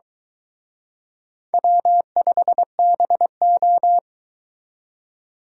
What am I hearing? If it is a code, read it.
Morse code, heard as a keyed tone, about 23 wpm, 700 Hz. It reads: W5BO